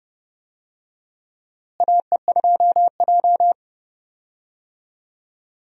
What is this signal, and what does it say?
Morse code, heard as a keyed tone, about 32 wpm, 700 Hz. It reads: AE2J
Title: USA Call Signs - 4 Characters - Repeated 3x - 30wpm